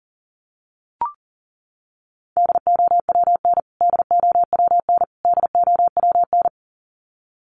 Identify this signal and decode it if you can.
Morse code, heard as a keyed tone, about 40 wpm, 700 Hz. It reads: DOWN DOWN DOWN
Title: Top 500 Words Repeated 40wpm